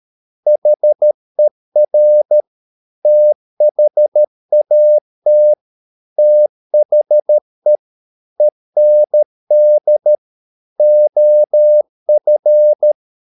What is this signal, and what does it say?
Morse code, heard as a keyed tone, about 13 wpm, 600 Hz. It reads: HER THAT THE END OF